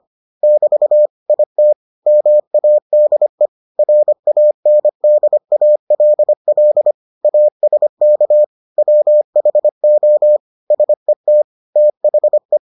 Morse code, heard as a keyed tone, about 25 wpm, 600 Hz. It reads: <BT> IT MADE RANDALL ASK WHO SET THE